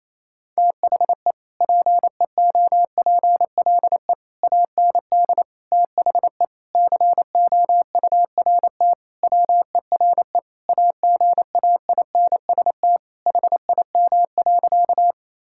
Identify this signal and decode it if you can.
Morse code, heard as a keyed tone, about 28 wpm, 700 Hz. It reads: THE PEOPLE AND THE COURT WERE AGAINST HIM.
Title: Sentences from Top 1600 Words 28wpm